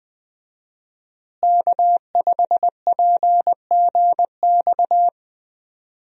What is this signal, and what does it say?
Morse code, heard as a keyed tone, about 20 wpm, 700 Hz. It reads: K5PGX
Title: USA Call Signs - 5 Characters - Repeated 2x - 20wpm